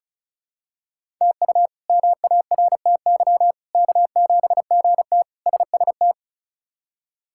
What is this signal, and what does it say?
Morse code, heard as a keyed tone, about 34 wpm, 700 Hz. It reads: TU MARTY K7GT SST
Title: K1USN SST QSO Practice - 35wpm